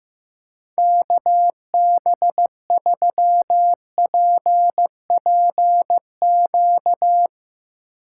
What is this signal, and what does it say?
Morse code, heard as a keyed tone, about 16 wpm, 700 Hz. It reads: KB3PPQ
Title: USA Call Signs - 2x Character Spacing - 15wpm